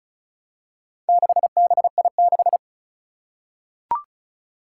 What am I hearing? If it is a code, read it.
Morse code, heard as a keyed tone, about 36 wpm, 700 Hz. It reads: 6BI6
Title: Four Challenging Letter-Number Combinations 35wpm